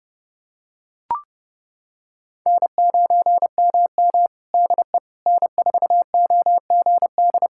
Morse code, heard as a keyed tone, about 32 wpm, 700 Hz. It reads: N9MM DE N4OGD
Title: K1USN SST QSO Practice 30wpm